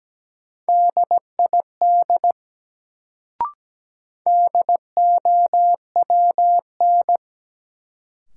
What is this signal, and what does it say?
Morse code, heard as a keyed tone, about 17 wpm, 700 Hz. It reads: DID DOWN
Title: Sets of 2 Words - New Word Drill - 17wpm